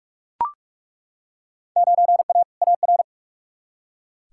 Morse code, heard as a keyed tone, about 41 wpm, 700 Hz. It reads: 9A AR
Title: Field Day Exchanges Encore - 45wpm